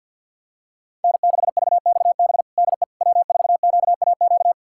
Morse code, heard as a keyed tone, about 47 wpm, 700 Hz. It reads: N6VXB DE W4QAQ
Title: USA Callsign DE Callsign - 5 Characters - 50wpm